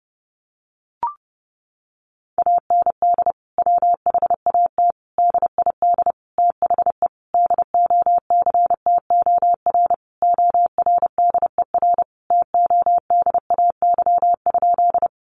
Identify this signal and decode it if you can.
Morse code, heard as a keyed tone, about 30 wpm, 700 Hz. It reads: AND WHAT DID THE DOCTOR ORDER TODAY?
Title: Sentences from Top 1500 Words 30wpm